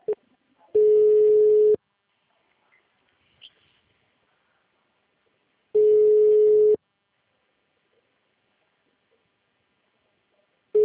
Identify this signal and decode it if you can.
Ringback tone (Europe, 425 Hz).
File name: out- -2017- - - .48008